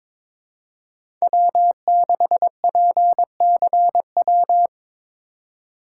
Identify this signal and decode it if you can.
Morse code, heard as a keyed tone, about 21 wpm, 700 Hz. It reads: W6PCW
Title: USA Call Signs - 5 Characters - 2x Character Spacing - 22wpm